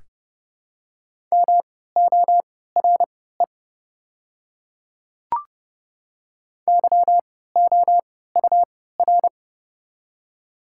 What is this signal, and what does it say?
Morse code, heard as a keyed tone, about 30 wpm, spaced as for 17 wpm, 700 Hz. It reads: MORE YOUR
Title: 3x Character Spacing - Top 100 Words - 30wpm